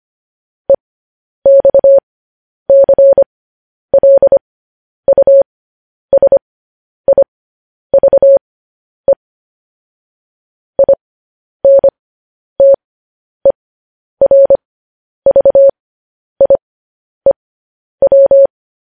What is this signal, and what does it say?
Morse code, heard as a keyed tone, about 25 wpm, spaced as for 10 wpm, 550 Hz. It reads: EXCLUSIVE INTERVIEW
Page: News Headlines in Morse Code at 10 WPM